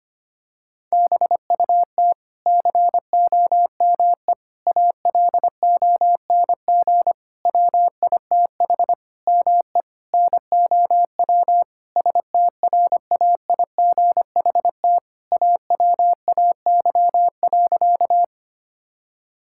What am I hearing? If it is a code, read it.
Morse code, heard as a keyed tone, about 25 wpm, 700 Hz. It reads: BUT COME ALONG WITH ME NOW STRAIGHT AWAY.
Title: Sentences from Top 1000 Words - Review Part 6 - 25wpm